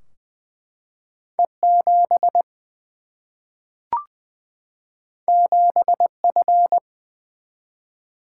Morse code, heard as a keyed tone, about 20 wpm, 700 Hz. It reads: E 7 7 F
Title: Single-Letter Single-Number 20wpm